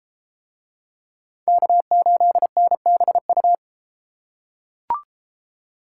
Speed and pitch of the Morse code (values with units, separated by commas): 30 wpm, 700 Hz